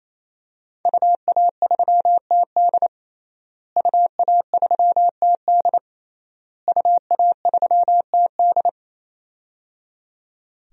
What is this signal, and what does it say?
Morse code, heard as a keyed tone, about 27 wpm, 700 Hz. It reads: UA3TD UA3TD UA3TD
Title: DXCC-100 Call Signs - 5 Characters - Repeated 3x - 28wpm